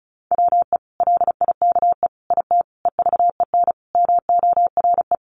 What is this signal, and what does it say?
Morse code, heard as a keyed tone, about 34 wpm, 700 Hz. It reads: WE LIKE IT EVEN MORE
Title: Sentences from Top 1100 Words 35wpm